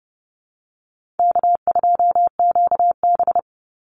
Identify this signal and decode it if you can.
Morse code, heard as a keyed tone, about 30 wpm, 700 Hz. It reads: K2QB
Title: Easy Call Signs Repeated 3x - 30wpm